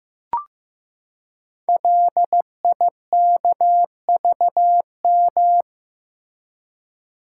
Morse code, heard as a keyed tone, about 15 wpm, 700 Hz. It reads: LIKVM
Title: Code Groups - 5 Letters - 15wpm